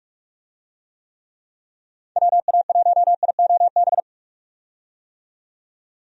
Morse code, heard as a keyed tone, about 48 wpm, 700 Hz. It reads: WA1IOB
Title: USA Call Signs - 6 Characters - 45wpm